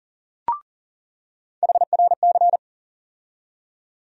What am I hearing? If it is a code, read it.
Morse code, heard as a keyed tone, about 40 wpm, 700 Hz. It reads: HRC
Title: USA Call Sign Postfixes - 3 Letters 40wpm